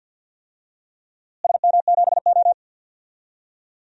Morse code, heard as a keyed tone, about 53 wpm, 700 Hz. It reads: SM7O
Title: DXCC-100 Call Signs - 4 Characters - Repeated 3x - 50wpm